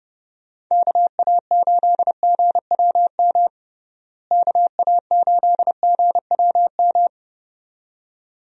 Morse code, heard as a keyed tone, about 32 wpm, 700 Hz. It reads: KA8GWM KA8GWM
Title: USA Call Signs - 6 Characters - Repeated 2x - 30wpm